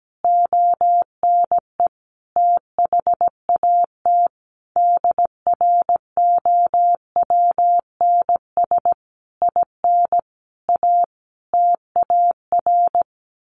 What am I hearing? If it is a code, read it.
Morse code, heard as a keyed tone, about 17 wpm, 700 Hz. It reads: ONE THAT DROWNS IN A TAR
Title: A Sound Of Thunder 17wpm